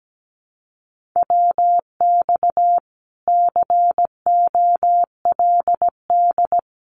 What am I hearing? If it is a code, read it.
Morse code, heard as a keyed tone, about 17 wpm, 700 Hz. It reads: WX COLD
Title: Common QSO Phrases Encore 17wpm